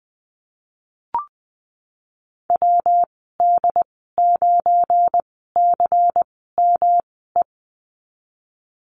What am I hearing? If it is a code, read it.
Morse code, heard as a keyed tone, about 20 wpm, spaced as for 15 wpm, 700 Hz. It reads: WD9CME
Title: 186 Lesson 30g - Letter b - Call Signs 20wpm